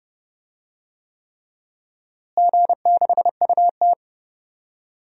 Morse code, heard as a keyed tone, about 31 wpm, 700 Hz. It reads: G6UT
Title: DXCC-100 Call Signs - 4 Characters - Repeated 3x - 30wpm